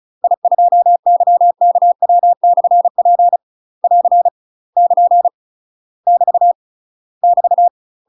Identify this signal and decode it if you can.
Morse code, heard as a keyed tone, about 36 wpm, 700 Hz. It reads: I2YKW/P <AR> <KN> <BT> <BT>
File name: UK MASTER QSOs TRAINING / 35WPM / UK MASTER QSOs 0022